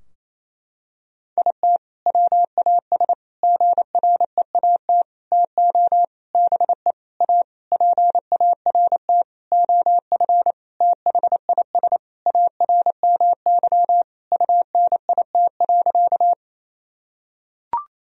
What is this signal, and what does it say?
Morse code, heard as a keyed tone, about 28 wpm, 700 Hz. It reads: IT WAS GREAT TO BE A PART OF THIS ARMY UNIT.
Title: Sentences from Top 1000 Words - Review Part 2 - 28wpm